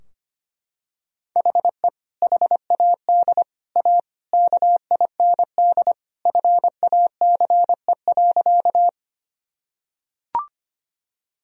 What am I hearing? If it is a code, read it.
Morse code, heard as a keyed tone, about 25 wpm, 700 Hz. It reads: HE HAD A KIND FACE.
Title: Sentences from Top 1000 Words - Review Part 2 - 25wpm